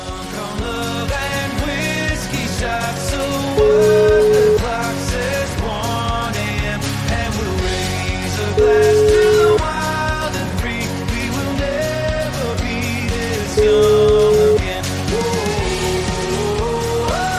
Ringback tone (Europe, 425 Hz), with noise.